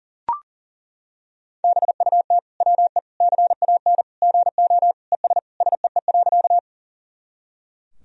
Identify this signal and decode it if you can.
Morse code, heard as a keyed tone, about 40 wpm, 700 Hz. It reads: BUT WE CAN GO ES SEE.
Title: Sentences from Top 100 Words and Abbreviations 40wpm